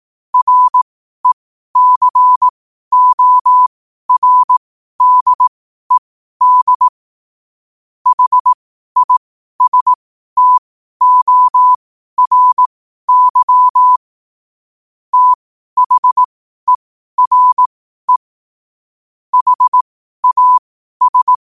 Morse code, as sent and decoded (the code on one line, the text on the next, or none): .-. -.-. --- .-. -.. -.. / .... .. ... - --- .-. -.-- / - .... .-. / .... .- ...
RECORDED HISTORY THERE HAS